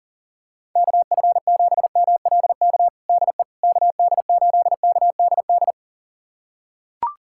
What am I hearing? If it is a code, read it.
Morse code, heard as a keyed tone, about 40 wpm, 700 Hz. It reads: KF7MLK DE KD8KDD
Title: USA Callsign DE Callsign 40wpm